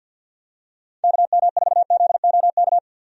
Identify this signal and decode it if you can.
Morse code, heard as a keyed tone, about 51 wpm, 700 Hz. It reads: KM4ZOK
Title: USA Call Signs - 6 Characters - 50wpm